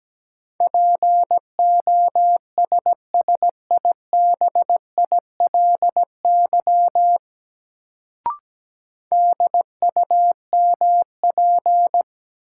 Morse code, heard as a keyed tone, about 17 wpm, 700 Hz. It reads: POSSIBILY DUMP
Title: Sentences from Top 1300 Words - New Word Drill Alt - 17wpm